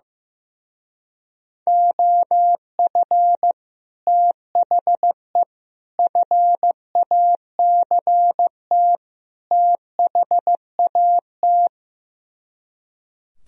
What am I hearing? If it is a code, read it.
Morse code, heard as a keyed tone, about 15 wpm, 700 Hz. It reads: OF THE FACT THAT
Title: Speed-Racing - Sets of 4 Words 15wpm